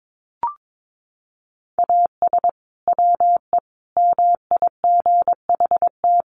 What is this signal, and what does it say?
Morse code, heard as a keyed tone, about 22 wpm, 700 Hz. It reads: AS WE MIGHT
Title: Sentences from Top 1100 Words 22wpm